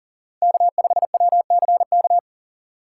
Morse code, heard as a keyed tone, about 40 wpm, 700 Hz. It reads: K5WCK
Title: USA Call Signs - 5 Characters - 4x Character Spacing - 40wpm